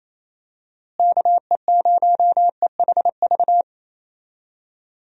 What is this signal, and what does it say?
Morse code, heard as a keyed tone, about 28 wpm, 700 Hz. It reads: KE0EHV
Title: USA Call Signs - 6 Characters - Repeated 2x - 28wpm